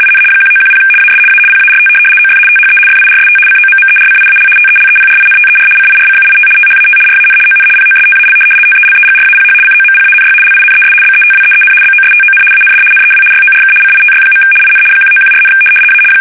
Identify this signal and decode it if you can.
RTTY, about 75 baud. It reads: IN 1910 AS A PART OF THE FIRST COMMERCIAL TELEPRINTER CIRCUIT THAT RAN ON POSTAL TELEGRAPH COMPANY LINES BETWEEN BOSTON AND NEW YORK CITY. IN 1914, A
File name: fsk850-75B / teletype-corp